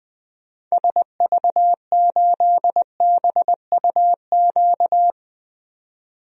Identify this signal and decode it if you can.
Morse code, heard as a keyed tone, about 20 wpm, 700 Hz. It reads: SV8BUQ
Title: DXCC-100 Call Signs - 6 Characters - 20wpm